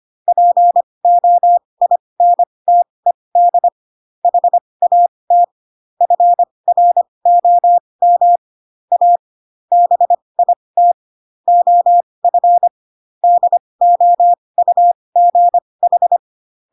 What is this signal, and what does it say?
Morse code, heard as a keyed tone, about 25 wpm, spaced as for 20 wpm, 700 Hz. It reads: POINTED HAT FROM A BIT OF DOUGH
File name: Pinocchio / Pinocchio ch8 15 25 ik1hge